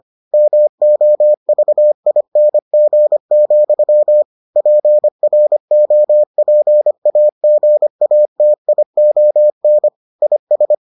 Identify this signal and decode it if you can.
Morse code, heard as a keyed tone, about 25 wpm, 600 Hz. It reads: MOVING, PROPAGATION IS